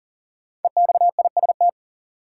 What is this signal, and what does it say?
Morse code, heard as a keyed tone, about 40 wpm, 700 Hz. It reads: EXIST
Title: Sentences from Top 1100 Words - New Word Drill Alt - 40wpm